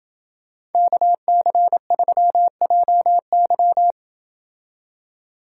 Morse code, heard as a keyed tone, about 28 wpm, 700 Hz. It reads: KC3JY